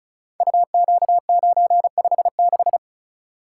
Morse code, heard as a keyed tone, about 36 wpm, 700 Hz. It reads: UQ956
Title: Code Groups - 5 Letter:Numbers - 35wpm